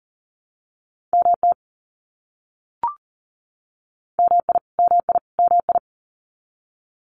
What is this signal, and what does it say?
Morse code, heard as a keyed tone, about 40 wpm, 700 Hz. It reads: MT MI MI MI
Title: US States Repeated 3x 40wpm